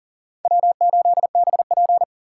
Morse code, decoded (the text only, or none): W8BP